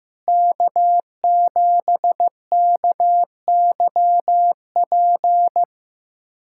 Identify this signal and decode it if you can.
Morse code, heard as a keyed tone, about 15 wpm, 700 Hz. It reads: K7KYP
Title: USA Callsign DE Callsign - 5 Characters - 15wpm